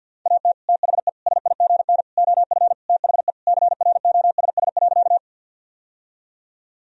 Morse code, heard as a keyed tone, about 50 wpm, 700 Hz. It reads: AT THE SIGN OF THE CROSS.
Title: Sentences from Top 1000 Words - Review Part 1 - 50wpm